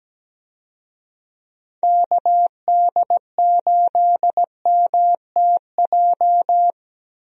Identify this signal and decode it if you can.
Morse code, heard as a keyed tone, about 17 wpm, 700 Hz. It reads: KD8MTJ
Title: USA Call Signs - 6 Characters - Repeated 3x - 17wpm